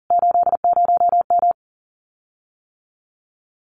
Morse code, heard as a keyed tone, about 37 wpm, 700 Hz. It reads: 80M
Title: Common QSO Phrases Encore 40wpm